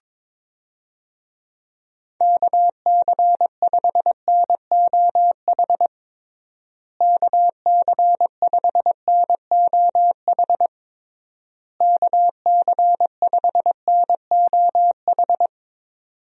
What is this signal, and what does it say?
Morse code, heard as a keyed tone, about 22 wpm, 700 Hz. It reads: KC5NOH KC5NOH KC5NOH
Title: USA Call Signs - 6 Characters - Repeated 3x - 22wpm